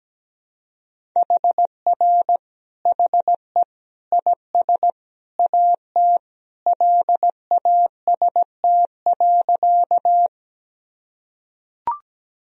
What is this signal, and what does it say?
Morse code, heard as a keyed tone, about 17 wpm, 700 Hz. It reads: HR HE IS AT LAST.
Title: Sentences from Top 300 Words and Abbreviations 17wpm